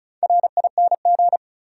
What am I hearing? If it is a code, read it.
Morse code, heard as a keyed tone, about 36 wpm, 700 Hz. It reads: RING
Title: Four Letter Words - 35wpm